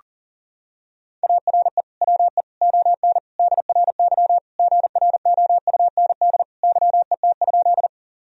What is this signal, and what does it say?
Morse code, heard as a keyed tone, about 37 wpm, 700 Hz. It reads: ARE WE ON DRY GROUND YET?
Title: Sentences from Top 1000 Words - Review Part 3 - 40wpm